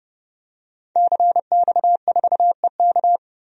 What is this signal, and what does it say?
Morse code, heard as a keyed tone, about 32 wpm, 700 Hz. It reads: CX4EK
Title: DXCC-100 Call Signs - 5 Characters - 30wpm